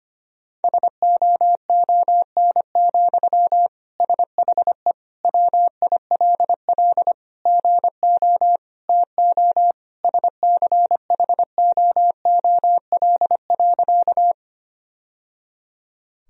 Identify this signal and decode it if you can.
Morse code, heard as a keyed tone, about 25 wpm, 700 Hz. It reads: SOON, SHE WILL GO TO SCHOOL.
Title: Sentences from Top 1000 Words - Review Part 1 - 25wpm